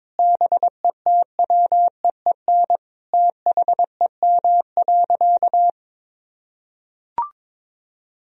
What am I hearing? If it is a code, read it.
Morse code, heard as a keyed tone, about 22 wpm, 700 Hz. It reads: BETWEEN THEM.
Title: Sentences from Top 400 Words and Abbreviations 22wpm